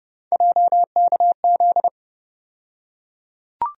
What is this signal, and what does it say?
Morse code, heard as a keyed tone, about 30 wpm, 700 Hz. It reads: JKZ
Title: USA Call Sign Postfixes - 3 Letters 30wpm